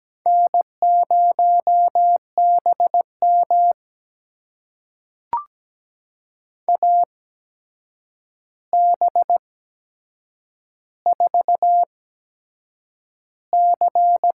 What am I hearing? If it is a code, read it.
Morse code, heard as a keyed tone, about 17 wpm, spaced as for 7 wpm, 700 Hz. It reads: N0BM AB4C
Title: USA Call Signs - 4 Characters - 8x Character Spacing - 17wpm